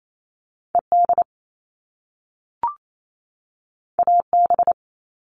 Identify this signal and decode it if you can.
Morse code, heard as a keyed tone, about 28 wpm, 700 Hz. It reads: ED AB
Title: All Cut Numbers - 2-Digit 28wpm